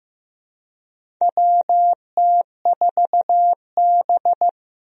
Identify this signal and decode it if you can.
Morse code, heard as a keyed tone, about 15 wpm, 700 Hz. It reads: WT4B
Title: USA Call Signs - 4 Characters - Repeated 3x - 15wpm